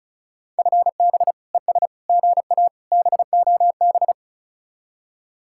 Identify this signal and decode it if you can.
Morse code, heard as a keyed tone, about 43 wpm, 700 Hz. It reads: FB ES GA BOB
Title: Speed-Racing Common QSO Phrases - 35wpm